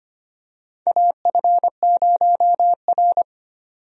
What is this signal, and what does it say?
Morse code, heard as a keyed tone, about 25 wpm, 700 Hz. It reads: AF0R
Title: USA Call Signs - 4 Characters - 25wpm